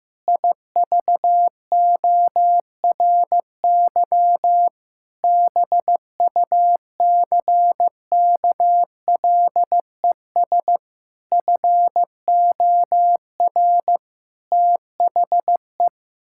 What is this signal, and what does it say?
Morse code, heard as a keyed tone, about 15 wpm, 700 Hz. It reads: IVORY BUCKLES FOR THE
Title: Pangram Challenge v2 - 15wpm